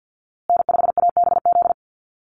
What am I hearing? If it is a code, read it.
Morse code, heard as a keyed tone, about 50 wpm, 700 Hz. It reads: N5ABZ
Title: USA Call Signs Repeated 2x 50wpm